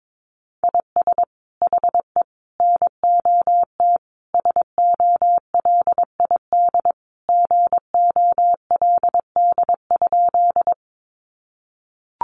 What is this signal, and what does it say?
Morse code, heard as a keyed tone, about 22 wpm, 700 Hz. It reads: IS HE NOT SOLID GOLD?
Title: Sentences from Top 1500 Words 22wpm